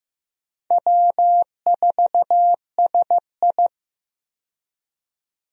Morse code, heard as a keyed tone, about 16 wpm, 700 Hz. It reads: W4SI
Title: USA Call Signs - 4 Characters - 15wpm